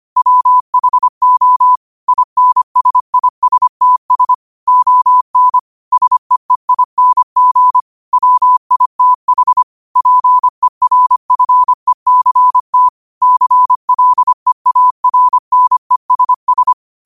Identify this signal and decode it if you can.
Morse code, heard as a keyed tone, about 25 wpm, 1000 Hz. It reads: WHO INSISTS ON SEEING WITH PERFECT CLEARNESS